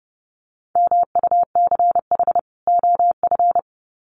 Morse code, heard as a keyed tone, about 30 wpm, 700 Hz. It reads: MUCH OF
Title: Sets of 2 Words 30wpm